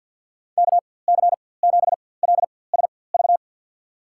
Morse code, heard as a keyed tone, about 52 wpm, spaced as for 23 wpm, 700 Hz. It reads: KC7LSV
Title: USA Call Signs - 6 Characters - 4x Character Spacing - 50wpm